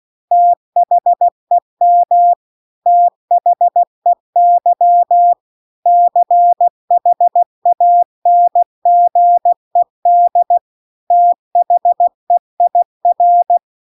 Morse code, heard as a keyed tone, about 16 wpm, 700 Hz. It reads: THEM THEY CHANGED THEIR
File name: CW-The-Big-Twoheaded-River-1-16WPM